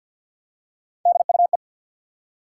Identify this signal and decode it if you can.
Morse code, heard as a keyed tone, about 49 wpm, 700 Hz. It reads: DUE
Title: Three Letter Words - 50wpm